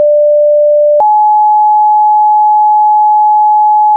Two-tone page, heard as a one-second tone, then a three-second tone, about 590 then 850 Hz, with noise. It is heard from 0.0 to 4.0 s.
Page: Fire Tone 14 mondern ringtone by Lazaro Jerde